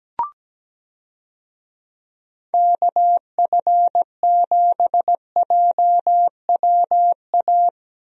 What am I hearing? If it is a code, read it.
Morse code, heard as a keyed tone, about 17 wpm, 700 Hz. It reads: KF7JWA
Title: USA Call Signs - 6 Characters - Repeated 2x - 17wpm